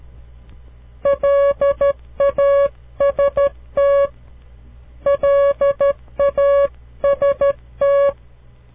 Morse code, heard as a keyed tone, about 12 wpm, 560 Hz. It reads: LAST LAST
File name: CW Learning Programs / farnsworth / unit 2